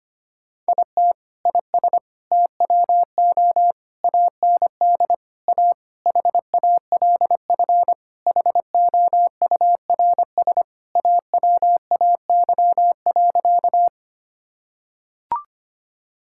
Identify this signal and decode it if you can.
Morse code, heard as a keyed tone, about 25 wpm, 700 Hz. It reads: IT IS TWO AND A HALF HOURS AWAY.
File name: practice / 25wpm / Sentences with 4-Word Phrases - 25wpm